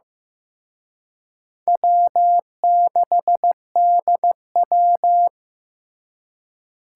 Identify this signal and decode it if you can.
Morse code, heard as a keyed tone, about 18 wpm, 700 Hz. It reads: W6DW
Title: Speed Racing - USA 1x2 Call Signs 15wpm